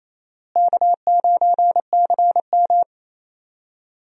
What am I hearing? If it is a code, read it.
Morse code, heard as a keyed tone, about 28 wpm, 700 Hz. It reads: K9CM
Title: USA Call Signs - 4 Characters - Repeated 2x - 28wpm